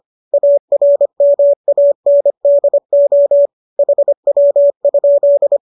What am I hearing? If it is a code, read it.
Morse code, heard as a keyed tone, about 25 wpm, 590 Hz. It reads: ARMANDO HW?